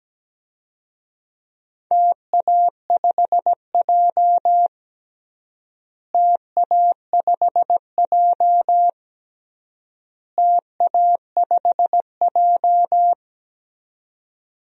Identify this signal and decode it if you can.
Morse code, heard as a keyed tone, about 17 wpm, 700 Hz. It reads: TA5J TA5J TA5J
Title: DXCC-100 Call Signs - 4 Characters - Repeated 3x - 17wpm